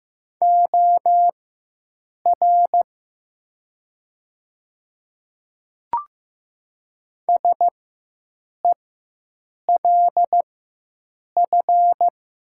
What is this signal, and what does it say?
Morse code, heard as a keyed tone, about 15 wpm, spaced as for 7 wpm, 700 Hz. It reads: OR SELF
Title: 4x Character Spacing - Top 500 Words - 15wpm